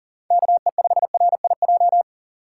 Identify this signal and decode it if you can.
Morse code, heard as a keyed tone, about 40 wpm, 700 Hz. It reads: KE5RIJ
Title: USA Call Signs - 6 Characters - 2x Character Spacing - 40wpm